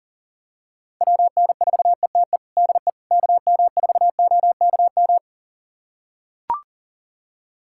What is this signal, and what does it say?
Morse code, heard as a keyed tone, about 40 wpm, 700 Hz. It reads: WN4ETE DE KM4OKM
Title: USA Callsign DE Callsign 40wpm